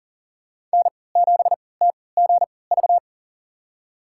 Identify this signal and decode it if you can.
Morse code, heard as a keyed tone, about 40 wpm, spaced as for 23 wpm, 700 Hz. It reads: N7TGV
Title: USA Call Signs - 5 Characters - 3x Character Spacing - 40wpm